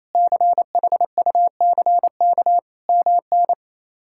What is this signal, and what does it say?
Morse code, heard as a keyed tone, about 27 wpm, 700 Hz. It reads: CHUCK MN
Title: K1USN SST QSO Practice - 28wpm